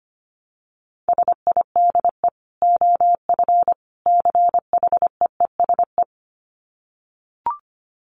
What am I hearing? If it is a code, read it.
Morse code, heard as a keyed tone, about 25 wpm, 700 Hz. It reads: SIDE OF CHEESE
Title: Dit Challenge 25wpm